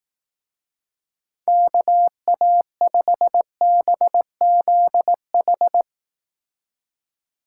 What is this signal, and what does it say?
Morse code, heard as a keyed tone, about 19 wpm, 700 Hz. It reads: KA5BZH
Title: Speed Racing - USA 2x3 Call Signs 15wpm